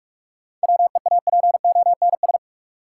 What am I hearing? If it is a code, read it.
Morse code, heard as a keyed tone, about 44 wpm, 700 Hz. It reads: WEAPONS
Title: Sets of 5 Words Encore - New Word Drill - 45wpm